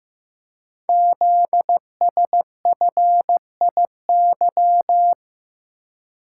Morse code, heard as a keyed tone, about 15 wpm, 700 Hz. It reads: ZSFIY
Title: Code Groups - 5 Letters - 15wpm